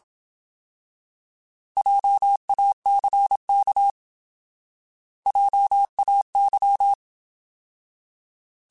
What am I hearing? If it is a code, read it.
Morse code, heard as a keyed tone, about 27 wpm, 800 Hz. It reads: JACK JAY
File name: Alphabetical / Alphabetical mp3s / Js @25